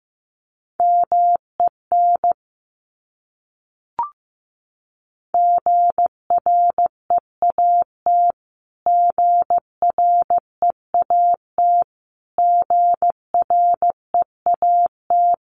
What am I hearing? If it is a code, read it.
Morse code, heard as a keyed tone, about 15 wpm, 700 Hz. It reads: MEN GREAT GREAT GREAT
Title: Top 300 Words Repeated 15wpm